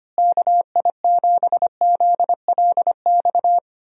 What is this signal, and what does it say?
Morse code, heard as a keyed tone, about 25 wpm, 700 Hz. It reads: KI7ZLX
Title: USA Call Signs - 6 Characters - Repeated 2x - 25wpm